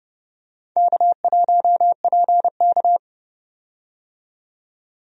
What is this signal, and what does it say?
Morse code, heard as a keyed tone, about 30 wpm, 700 Hz. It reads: K1PK
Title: USA Call Signs - 4 Characters - 30wpm